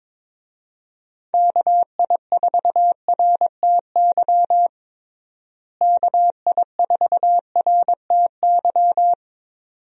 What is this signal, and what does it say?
Morse code, heard as a keyed tone, about 22 wpm, 700 Hz. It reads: KI4RTY KI4RTY
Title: USA Call Signs - 6 Characters - Repeated 2x - 22wpm